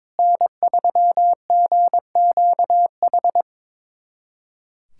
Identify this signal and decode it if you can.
Morse code, heard as a keyed tone, about 22 wpm, 700 Hz. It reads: N3GQH
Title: USA Callsign DE Callsign - 5 Characters - 22wpm